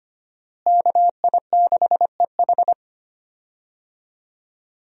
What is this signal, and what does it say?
Morse code, heard as a keyed tone, about 25 wpm, 700 Hz. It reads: KI6EH
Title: USA Call Signs - 5 Characters - Repeated 3x - 25wpm